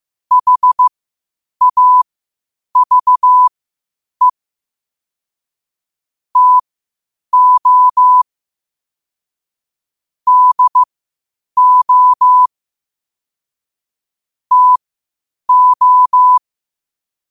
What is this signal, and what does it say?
Morse code, heard as a keyed tone, about 15 wpm, spaced as for 8 wpm, 1000 Hz. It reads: HAVE TO DO TO
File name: qotdfiles / 8wpm